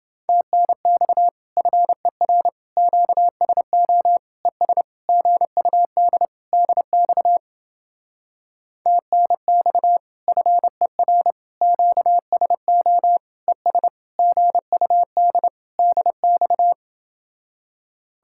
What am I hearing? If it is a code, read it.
Morse code, heard as a keyed tone, about 29 wpm, 700 Hz. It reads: TNX FER QSO ES GUD DX TNX FER QSO ES GUD DX